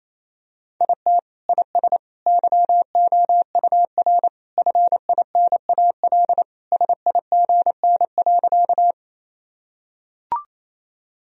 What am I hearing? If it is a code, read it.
Morse code, heard as a keyed tone, about 28 wpm, 700 Hz. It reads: IT IS YOUR FINAL SIGN.
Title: Sentences from Top 1000 Words - Review Part 2 - 28wpm